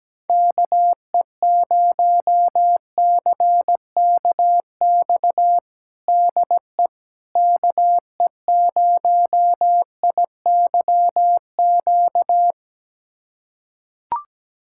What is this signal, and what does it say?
Morse code, heard as a keyed tone, about 17 wpm, 700 Hz. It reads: KE0CKX DE KE0IYQ
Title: USA Callsign DE Callsign - 6 Characters - 17wpm